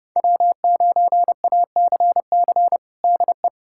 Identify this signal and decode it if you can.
Morse code, heard as a keyed tone, about 30 wpm, 700 Hz. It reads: W9ACC DE
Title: USA Callsign DE Callsign - 5 Characters - 30wpm